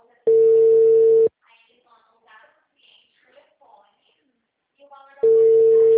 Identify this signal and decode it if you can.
Ringback tone (China, 450 Hz), with speech.
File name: out- -2004- - - .34